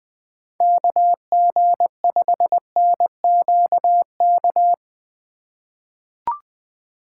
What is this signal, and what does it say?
Morse code, heard as a keyed tone, about 20 wpm, 700 Hz. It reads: KG5NQK
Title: USA Call Signs - 6 Characters - 3x Character Spacing - 20wpm